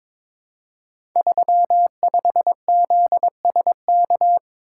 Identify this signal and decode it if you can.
Morse code, heard as a keyed tone, about 22 wpm, 700 Hz. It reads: 35ZSK
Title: Code Groups - 5 Letter:Numbers - 22wpm